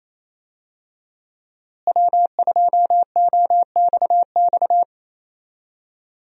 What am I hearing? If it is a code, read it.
Morse code, heard as a keyed tone, about 28 wpm, 700 Hz. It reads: W2OXX
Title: USA Call Signs - 5 Characters - Repeated 2x - 28wpm